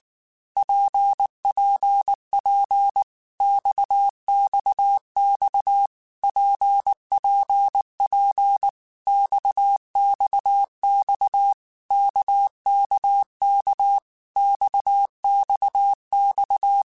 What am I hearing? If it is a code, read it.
Morse code, heard as a keyed tone, about 19 wpm, 800 Hz. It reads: PPP XXX PPP XXX KKK XXX
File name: Basic Letter Groups By7Groups / RKPX / RKPX@18